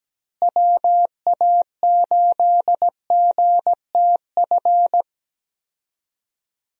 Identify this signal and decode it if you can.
Morse code, heard as a keyed tone, about 17 wpm, 700 Hz. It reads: WA8GTF
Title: USA Call Signs - 6 Characters - Repeated 3x - 17wpm